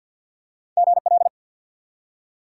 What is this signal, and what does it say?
Morse code, heard as a keyed tone, about 48 wpm, 700 Hz. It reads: GL
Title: Top 180 QSO Elements 50wpm